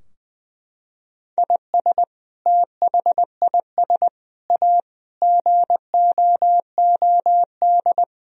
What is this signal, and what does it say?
Morse code, heard as a keyed tone, about 20 wpm, 700 Hz. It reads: IS THIS A GOOD
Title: Sentences from Top 1000 Words - Review Part 3 - 20wpm